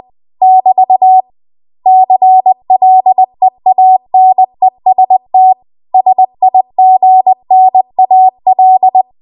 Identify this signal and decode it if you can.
Morse code, heard as a keyed tone, about 20 wpm, 750 Hz. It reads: <BT> CLEANEST SIGNAL
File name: Archive / 20 WPM / 20WPM